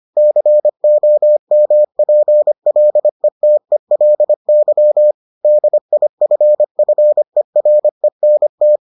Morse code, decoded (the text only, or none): COMPLETELY DIFFERENT